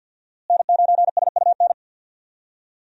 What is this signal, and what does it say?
Morse code, heard as a keyed tone, about 49 wpm, 700 Hz. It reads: N9SUN
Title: USA Call Signs - 5 Characters - Repeated 3x - 50wpm